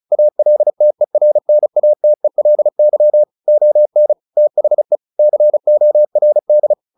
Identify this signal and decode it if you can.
Morse code, heard as a keyed tone, about 34 wpm, 600 Hz. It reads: ALTERNATELY ON THE CORD
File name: the old man and the sea / the old man and the sea 1 35wpm